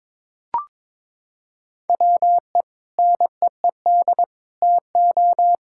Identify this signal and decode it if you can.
Morse code, heard as a keyed tone, about 22 wpm, 700 Hz. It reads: WE NEED TO
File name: practice / 22wpm / Sentences with 2-Word Phrases - 22wpm